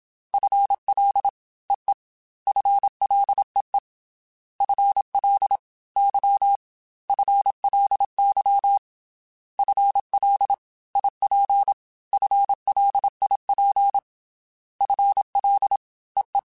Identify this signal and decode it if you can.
Morse code, heard as a keyed tone, about 27 wpm, 800 Hz. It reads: FL EE FLEE FL Y FLY FL IP FLIP FL EE
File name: FL / FL mp3s / Fast FL @25